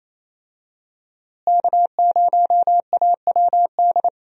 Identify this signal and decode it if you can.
Morse code, heard as a keyed tone, about 28 wpm, 700 Hz. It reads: K0AWD
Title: USA Call Signs - 5 Characters - Repeated 2x - 28wpm